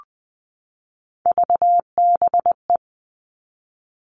Morse code, heard as a keyed tone, about 20 wpm, 700 Hz. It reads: VBE
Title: Three Challenging Letter-Number Combinations 20wpm